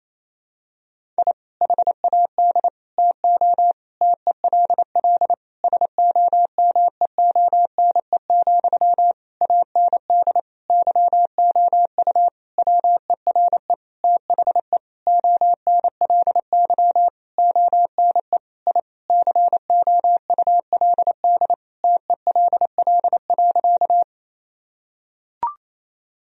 Morse code, heard as a keyed tone, about 28 wpm, 700 Hz. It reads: I HAD TO TELL SOMEONE, AND YOU WERE THE ONLY ONE I COULD TELL.
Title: Sentences from Top 1000 Words - Review Part 4 - 28wpm